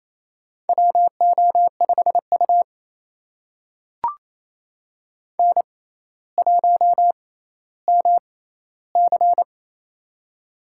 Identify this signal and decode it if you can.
Morse code, heard as a keyed tone, about 28 wpm, spaced as for 14 wpm, 700 Hz. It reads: WO5U N1MC
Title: USA Call Signs - 4 Characters - 6x Character Spacing - 28wpm